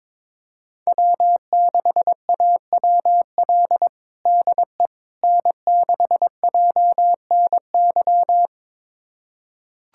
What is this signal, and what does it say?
Morse code, heard as a keyed tone, about 22 wpm, 700 Hz. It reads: W6AWL DE N6JNY
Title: USA Callsign DE Callsign - 5 Characters - 22wpm